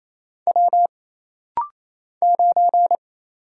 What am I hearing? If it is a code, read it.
Morse code, heard as a keyed tone, about 28 wpm, 700 Hz. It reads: W 9
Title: Single Letter-Number - Rapid-Fire 28wpm